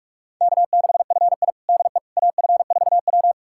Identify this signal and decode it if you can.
Morse code, heard as a keyed tone, about 41 wpm, 700 Hz. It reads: K6FI DE AF4W
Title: USA Callsign DE Callsign - 4 Characters - 45wpm